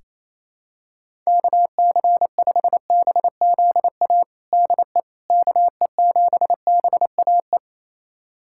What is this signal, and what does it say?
Morse code, heard as a keyed tone, about 28 wpm, 700 Hz. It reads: KC5BZA DE KE7BAE
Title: USA Callsign DE Callsign 28wpm